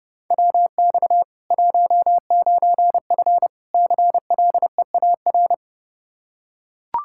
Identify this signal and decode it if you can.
Morse code, heard as a keyed tone, about 30 wpm, 700 Hz. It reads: WX 19F CLEAR
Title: Practice QSO - Exchange 2 - Info - 30wpm